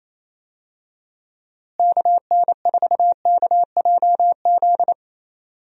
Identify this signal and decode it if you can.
Morse code, heard as a keyed tone, about 28 wpm, 700 Hz. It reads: KN4KJZ
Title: USA Call Signs - 6 Characters - Repeated 2x - 28wpm